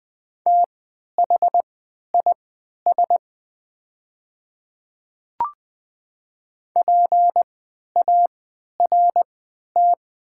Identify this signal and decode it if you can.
Morse code, heard as a keyed tone, about 20 wpm, spaced as for 11 wpm, 700 Hz. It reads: THIS PART
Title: 3x Character Spacing - Top 200 Words - 20wpm